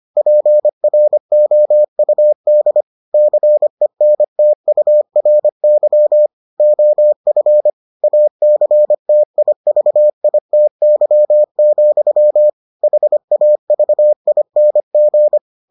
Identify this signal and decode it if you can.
Morse code, heard as a keyed tone, about 25 wpm, 600 Hz. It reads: PROUD CENTURY OF ACTIVITY, HAVING